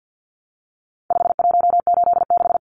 Morse code, heard as a keyed tone, about 50 wpm, 700 Hz. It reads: 5186
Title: Numbers - 4-Digit 50wpm